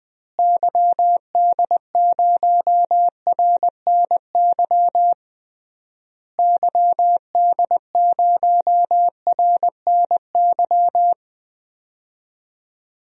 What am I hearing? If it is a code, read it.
Morse code, heard as a keyed tone, about 20 wpm, 700 Hz. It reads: YD0RNY YD0RNY
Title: DXCC-100 Call Signs - 6 Characters - Repeated 3x - 20wpm